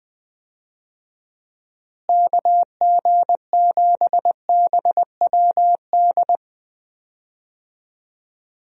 Morse code, heard as a keyed tone, about 21 wpm, 700 Hz. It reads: KG7BWD
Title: USA Call Signs - 6 Characters - Repeated 3x - 20wpm